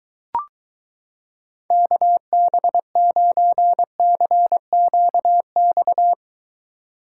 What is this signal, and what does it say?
Morse code, heard as a keyed tone, about 23 wpm, 700 Hz. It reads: KB9CQX